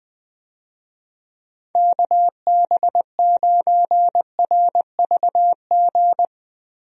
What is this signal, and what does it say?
Morse code, heard as a keyed tone, about 20 wpm, 700 Hz. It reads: KB9RVG
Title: USA Call Signs - 6 Characters - Repeated 2x - 20wpm